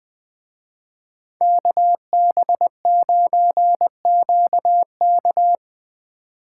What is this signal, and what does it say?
Morse code, heard as a keyed tone, about 22 wpm, 700 Hz. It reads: KB9QK
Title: Speed Racing - USA Call Signs 20wpm